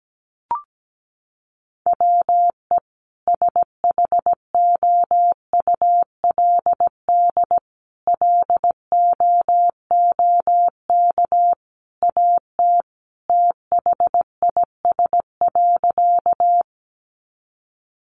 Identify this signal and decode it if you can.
Morse code, heard as a keyed tone, about 17 wpm, 700 Hz. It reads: WE SHOULD LOOK AT THIS.
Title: Sentences from Top 300 Words 17wpm